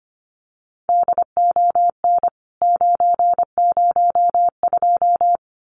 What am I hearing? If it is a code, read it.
Morse code, heard as a keyed tone, about 25 wpm, 700 Hz. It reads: DON 902
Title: CWT Exchange 25wpm